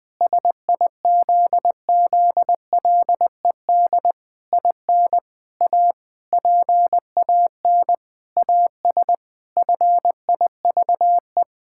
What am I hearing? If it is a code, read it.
Morse code, heard as a keyed tone, about 21 wpm, 700 Hz. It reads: SIZZLED IN A PAN AS FIVE
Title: Pangram Challenge v2 - 20wpm